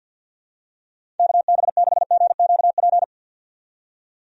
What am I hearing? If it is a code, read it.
Morse code, heard as a keyed tone, about 50 wpm, 700 Hz. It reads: KB6GQP
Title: USA Call Signs - 6 Characters - 50wpm